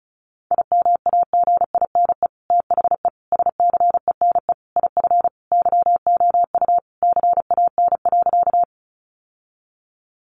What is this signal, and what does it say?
Morse code, heard as a keyed tone, about 34 wpm, 700 Hz. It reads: IMAGINE THE SCENE IF YOU CAN.
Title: Sentences from Top 1400 Words 35wpm